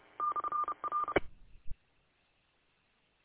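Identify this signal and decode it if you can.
Morse code, heard as a keyed tone, about 30 wpm, 1200 Hz. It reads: /R
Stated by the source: Morse code from an audio archive of station w8um.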